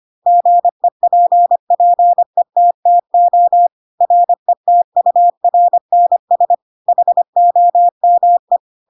Morse code, heard as a keyed tone, about 25 wpm, 700 Hz. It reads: GEPPETTO RETURNS HOME
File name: Pinocchio / Pinocchio ch7 25 25 ik1hge